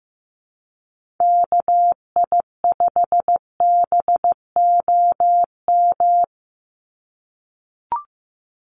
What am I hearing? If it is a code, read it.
Morse code, heard as a keyed tone, about 15 wpm, 700 Hz. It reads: KI5BOM
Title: USA Call Signs 15wpm